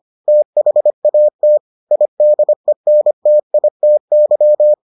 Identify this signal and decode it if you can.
Morse code, heard as a keyed tone, about 25 wpm, 600 Hz. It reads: THAT IDENTITY